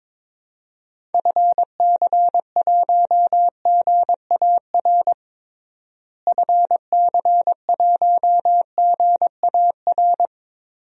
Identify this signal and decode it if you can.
Morse code, heard as a keyed tone, about 21 wpm, 700 Hz. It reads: FC1GAR FC1GAR
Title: DXCC-100 Call Signs - 6 Characters - Repeated 3x - 22wpm